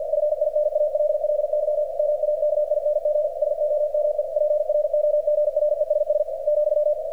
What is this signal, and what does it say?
Morse code, heard as a keyed tone, about 48 wpm, 600 Hz, with radio noise. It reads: FINALLY MAKING SOME PROGRESS ,
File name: archived / CRSnet